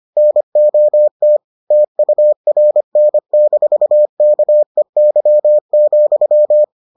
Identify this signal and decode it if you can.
Morse code, heard as a keyed tone, about 25 wpm, 600 Hz. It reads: NOT TURN-KEY,